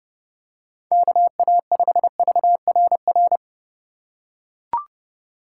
Morse code, heard as a keyed tone, about 32 wpm, 700 Hz. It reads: KA5VRR
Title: USA Call Signs - 6 Characters - 4x Character Spacing - 30wpm